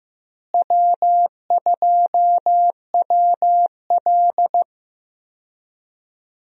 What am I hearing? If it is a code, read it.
Morse code, heard as a keyed tone, about 16 wpm, 700 Hz. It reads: W2WL
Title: USA Call Signs - 4 Characters - Repeated 3x - 15wpm